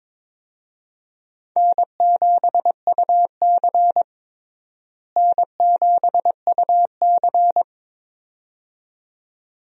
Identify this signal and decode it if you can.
Morse code, heard as a keyed tone, about 22 wpm, 700 Hz. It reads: N7UC N7UC
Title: USA Call Signs - 4 Characters - Repeated 2x - 22wpm